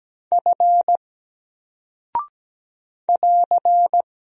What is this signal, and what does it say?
Morse code, heard as a keyed tone, about 17 wpm, 700 Hz. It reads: F <AR>
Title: Single Character (Letter:Number:Punctuation:Prosign) 17wpm